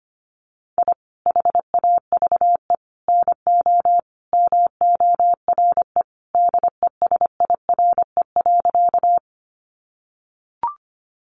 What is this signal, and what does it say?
Morse code, heard as a keyed tone, about 25 wpm, 700 Hz. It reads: I HAVE NO MORE DESIRE.
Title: Sentences from Top 1100 Words 25wpm